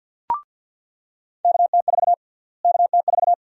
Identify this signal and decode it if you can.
Morse code, heard as a keyed tone, about 46 wpm, 700 Hz. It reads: KT4 KT4
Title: USA Call Sign Prefixes - Repeated 2x - 50wpm